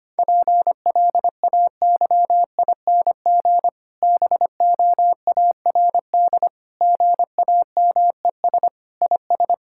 Morse code, heard as a keyed tone, about 25 wpm, 700 Hz. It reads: PLAYING BOARD GAMES IS
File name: practice / 25wpm / Difficult Sentences with Hints - 25wpm